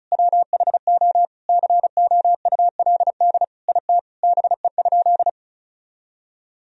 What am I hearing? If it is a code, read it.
Morse code, heard as a keyed tone, about 34 wpm, 700 Hz. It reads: WHO COULD IT BE?
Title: Sentences from Top 1000 Words - Review Part 2 - 35wpm